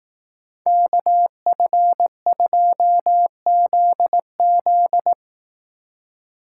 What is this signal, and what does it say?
Morse code, heard as a keyed tone, about 18 wpm, 700 Hz. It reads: KF2ZZ
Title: Speed Racing - USA 2x2 Call Signs 15wpm